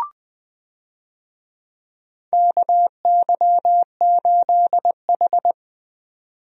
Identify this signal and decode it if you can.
Morse code, heard as a keyed tone, about 20 wpm, 700 Hz. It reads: KY8H
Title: USA Call Signs - 4 Characters - 20wpm